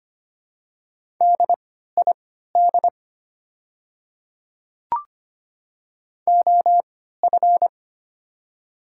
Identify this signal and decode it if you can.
Morse code, heard as a keyed tone, about 25 wpm, spaced as for 14 wpm, 690 Hz. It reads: DID OF
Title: 3x Character Spacing - Top 100 Words - 25wpm